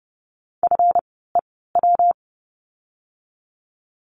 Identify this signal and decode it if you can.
Morse code, heard as a keyed tone, about 30 wpm, spaced as for 17 wpm, 700 Hz. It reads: FEW
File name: course / 30:10wpm / 230 Lesson 37f - Letter x - Words 30wpm